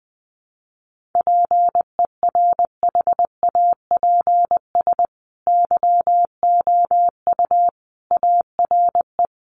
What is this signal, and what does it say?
Morse code, heard as a keyed tone, about 20 wpm, 700 Hz. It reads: PERHAPS YOU ARE
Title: Sentences from Top 1400 Words 20wpm